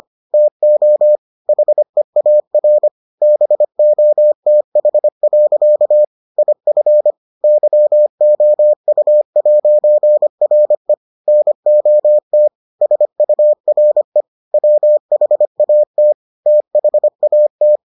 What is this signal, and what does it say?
Morse code, heard as a keyed tone, about 25 wpm, 600 Hz. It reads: TO HEAR BOTH. IF YOU'RE NOT SURE WHAT THAT